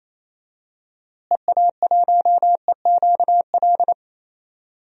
Morse code, heard as a keyed tone, about 27 wpm, 700 Hz. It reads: EA1EQL
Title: DXCC-100 Call Signs - 6 Characters - Repeated 2x - 28wpm